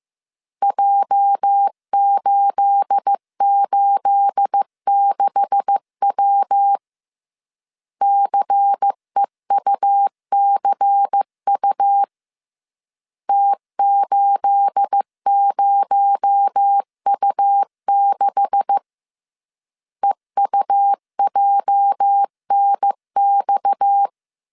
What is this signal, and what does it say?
Morse code, heard as a keyed tone, about 15 wpm, 800 Hz. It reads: J886W CEUCU T80U6 EUJNX
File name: CW / DET / 1 9 1500C S A 14wpm 3222